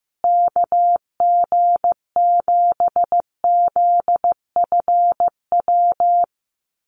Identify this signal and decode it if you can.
Morse code, heard as a keyed tone, about 15 wpm, 700 Hz. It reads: KG7ZFW
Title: USA Call Signs Repeated 2x 15wpm